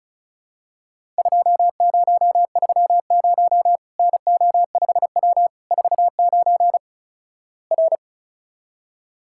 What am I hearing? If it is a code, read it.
Morse code, heard as a keyed tone, about 36 wpm, 700 Hz. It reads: 2030 NO5W 49 R
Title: Worked All Europe DX Contest - QTC Exchange - 35wpm